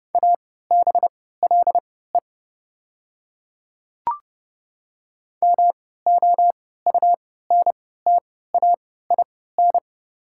Morse code, heard as a keyed tone, about 30 wpm, spaced as for 17 wpm, 690 Hz. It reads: ABLE MOUNTAIN
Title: 3x Character Spacing - Top 500 Words - 30wpm